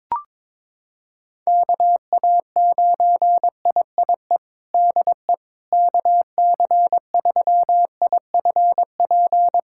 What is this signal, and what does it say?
Morse code, heard as a keyed tone, about 22 wpm, 700 Hz. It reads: KA9IIE DE KC3IFP
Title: USA Callsign DE Callsign 22wpm